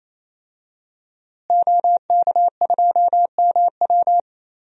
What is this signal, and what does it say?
Morse code, heard as a keyed tone, about 27 wpm, 700 Hz. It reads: OK2MW
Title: DXCC-100 Call Signs - 5 Characters - Repeated 3x - 28wpm